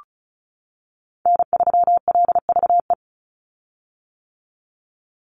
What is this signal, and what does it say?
Morse code, heard as a keyed tone, about 34 wpm, 700 Hz. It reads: N3LVE
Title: USA Call Signs 35wpm